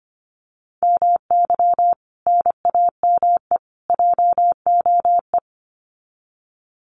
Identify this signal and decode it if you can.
Morse code, heard as a keyed tone, about 25 wpm, 700 Hz. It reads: MY NAME JOE
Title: Common QSO Phrases 25wpm